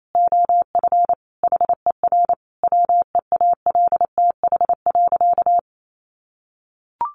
Morse code, heard as a keyed tone, about 28 wpm, 700 Hz. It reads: OF HER WEALTH.
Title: Sentences from Top 1500 Words 28wpm